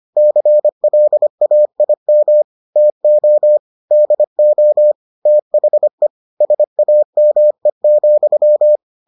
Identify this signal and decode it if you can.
Morse code, heard as a keyed tone, about 25 wpm, 600 Hz. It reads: CLAIM TO DO THE SAME,